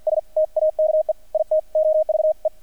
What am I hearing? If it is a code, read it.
Morse code, heard as a keyed tone, about 50 wpm, 620 Hz, with radio noise. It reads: S TAKE IT OVE
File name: archived / CRSnet